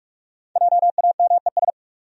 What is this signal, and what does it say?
Morse code, heard as a keyed tone, about 46 wpm, 700 Hz. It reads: JAMES
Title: USA Common Names - Max 5 Letters - 45wpm